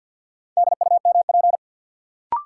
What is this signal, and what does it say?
Morse code, heard as a keyed tone, about 53 wpm, 700 Hz. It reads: DUMP
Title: Sentences from Top 1300 Words - New Word Drill Alt - 50wpm